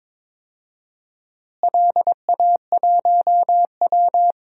Morse code, heard as a keyed tone, about 22 wpm, 700 Hz. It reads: LA1W
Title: DXCC-100 Call Signs - 4 Characters - Repeated 3x - 22wpm